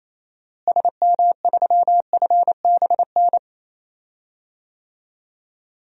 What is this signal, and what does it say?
Morse code, heard as a keyed tone, about 28 wpm, 700 Hz. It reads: SM3FBN
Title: DXCC-100 Call Signs - 6 Characters - Repeated 2x - 28wpm